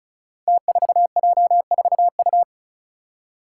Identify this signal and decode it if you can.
Morse code, heard as a keyed tone, about 34 wpm, 700 Hz. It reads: T4J4U
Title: Five Challenging Letter-Number Combinations 35wpm